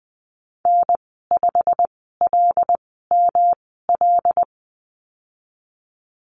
Morse code, heard as a keyed tone, about 20 wpm, spaced as for 15 wpm, 700 Hz. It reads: N5LML
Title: 147 Lesson 24g - Letter y - Call Signs 20wpm